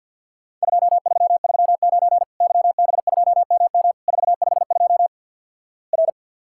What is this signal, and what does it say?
Morse code, heard as a keyed tone, about 50 wpm, 700 Hz. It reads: 2339 YB2MM 452 R
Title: Worked All Europe DX Contest - QTC Exchange with Repeats - 50wpm